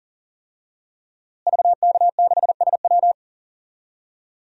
Morse code, heard as a keyed tone, about 40 wpm, 700 Hz. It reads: VK6SW
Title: DXCC-100 Call Signs - 5 Characters - Repeated 2x - 40wpm